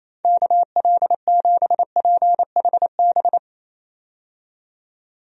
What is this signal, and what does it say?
Morse code, heard as a keyed tone, about 28 wpm, 700 Hz. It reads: KL7PHB
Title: DXCC-100 Call Signs - 6 Characters - 28wpm